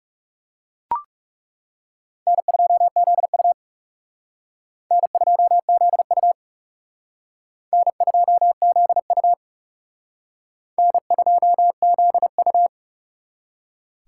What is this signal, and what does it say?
Morse code, heard as a keyed tone, about 38 wpm, 700 Hz. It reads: N2ZU N2ZU N2ZU N2ZU